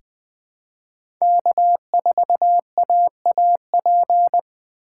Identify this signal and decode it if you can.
Morse code, heard as a keyed tone, about 20 wpm, 700 Hz. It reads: K4AAP
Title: USA Call Signs - 4x Character Spacing - 20wpm